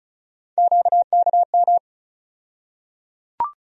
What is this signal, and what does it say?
Morse code, heard as a keyed tone, about 34 wpm, 700 Hz. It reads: QKM
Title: USA Call Sign Postfixes - 3 Letters 35wpm